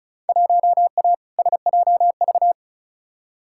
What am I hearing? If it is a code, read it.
Morse code, heard as a keyed tone, about 36 wpm, 700 Hz. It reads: 1A SJV
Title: Field Day Exchanges Encore - 35wpm